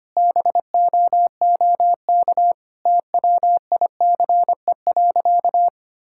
Morse code, heard as a keyed tone, about 25 wpm, 700 Hz. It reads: BOOK TWICE.
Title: Sentences from Top 1600 Words 25wpm